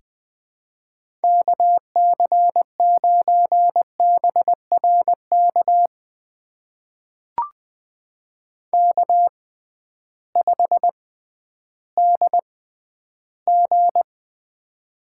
Morse code, heard as a keyed tone, about 20 wpm, spaced as for 12 wpm, 700 Hz. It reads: KC9BRK K5DG
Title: USA Call Signs - 6x Character Spacing - 20wpm